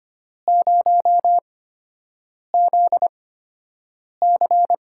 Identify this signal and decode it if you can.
Morse code, heard as a keyed tone, about 25 wpm, spaced as for 7 wpm, 700 Hz. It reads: 0ZC
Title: USA Call Signs - 5 Characters - 8x Character Spacing - 25wpm